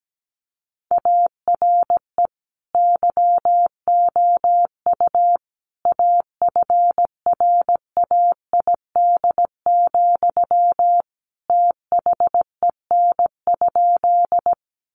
Morse code, heard as a keyed tone, about 17 wpm, 700 Hz. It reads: ARE YOU AFRAID, THEN?
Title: Sentences from Top 1500 Words 17wpm